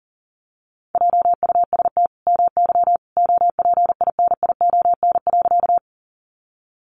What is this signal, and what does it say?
Morse code, heard as a keyed tone, about 40 wpm, 700 Hz. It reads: JUST MY OPINION.
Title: Sentences from Top 1200 Words 40wpm